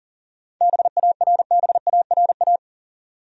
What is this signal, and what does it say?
Morse code, heard as a keyed tone, about 40 wpm, 700 Hz. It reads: BARBARA
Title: USA Common Names - 40wpm